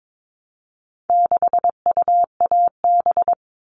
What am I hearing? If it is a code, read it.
Morse code, heard as a keyed tone, about 22 wpm, 700 Hz. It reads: 6UAB
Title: All Cut Numbers - 4-Digit 22wpm